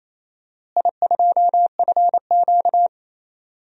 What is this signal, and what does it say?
Morse code, heard as a keyed tone, about 28 wpm, 700 Hz. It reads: I2FQ
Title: DXCC-100 Call Signs - 4 Characters - Repeated 3x - 28wpm